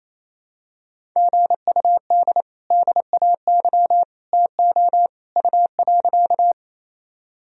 Morse code, heard as a keyed tone, about 28 wpm, 700 Hz. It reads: GUD DAY TO U.
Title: Sentences from Top 200 Words and Abbreviations 28wpm